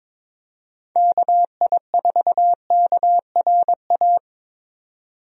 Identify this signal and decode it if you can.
Morse code, heard as a keyed tone, about 21 wpm, 700 Hz. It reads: KI4KRA
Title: USA Call Signs - 6 Characters - Repeated 3x - 22wpm